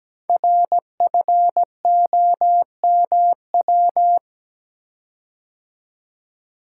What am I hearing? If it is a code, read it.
Morse code, heard as a keyed tone, about 17 wpm, 700 Hz. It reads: RFOMW
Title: Code Groups - 5 Letters - 17wpm